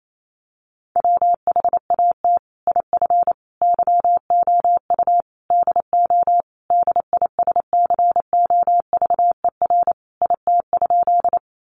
Morse code, heard as a keyed tone, about 28 wpm, 700 Hz. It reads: WHAT IF YOU DO DISCOVER IT?
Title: Sentences from Top 1100 Words 28wpm